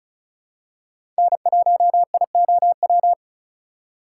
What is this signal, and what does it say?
Morse code, heard as a keyed tone, about 36 wpm, 700 Hz. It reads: N1IOW
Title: USA Call Signs - 5 Characters - 4x Character Spacing - 35wpm